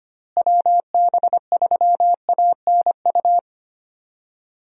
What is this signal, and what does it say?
Morse code, heard as a keyed tone, about 25 wpm, 700 Hz. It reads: WB3ANU
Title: USA Call Signs - 6 Characters - Repeated 2x - 25wpm